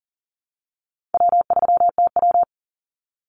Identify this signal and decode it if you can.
Morse code, heard as a keyed tone, about 40 wpm, 700 Hz. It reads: W3TW
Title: Easy Call Signs Repeated 3x - 40wpm